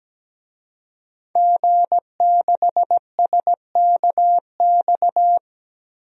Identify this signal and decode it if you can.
Morse code, heard as a keyed tone, about 17 wpm, 700 Hz. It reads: G6SKX
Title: DXCC-100 Call Signs - 5 Characters - Repeated 2x - 17wpm